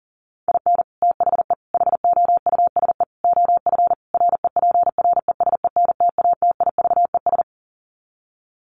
Morse code, heard as a keyed tone, about 40 wpm, 700 Hz. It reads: IN THE HOUSE OF REPRESENTATIVES
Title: Sets of 5 Words Encore 40wpm